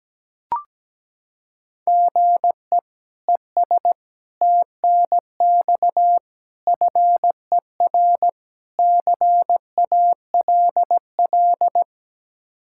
Morse code, heard as a keyed tone, about 17 wpm, 700 Hz. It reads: GE ES TNX FER CALL
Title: K1USN SST QSO Practice 17wpm